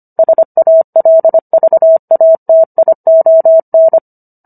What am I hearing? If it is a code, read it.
Morse code, heard as a keyed tone, about 25 wpm, 650 Hz. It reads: SALVATION